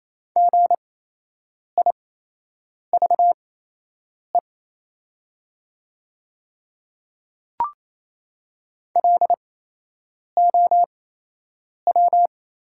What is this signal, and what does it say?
Morse code, heard as a keyed tone, about 28 wpm, spaced as for 8 wpm, 700 Hz. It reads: GIVE LOW
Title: 8x Character Spacing - Top 200 Words - 28wpm